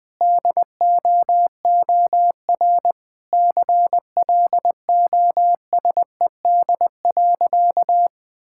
Morse code, heard as a keyed tone, about 20 wpm, 700 Hz. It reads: DOOR CLOSED.
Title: Sentences with 3-Word Phrases - 20wpm